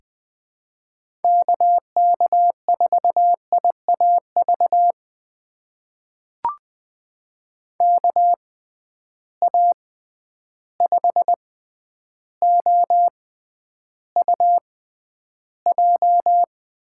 Morse code, heard as a keyed tone, about 20 wpm, spaced as for 10 wpm, 700 Hz. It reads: KK4IAV KA5OUJ
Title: USA Call Signs - 6 Characters - 6x Character Spacing - 20wpm